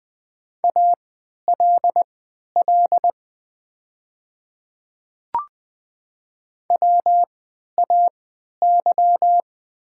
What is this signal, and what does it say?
Morse code, heard as a keyed tone, about 20 wpm, spaced as for 11 wpm, 700 Hz. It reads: ALL WAY
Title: 3x Character Spacing - Top 100 Words - 20wpm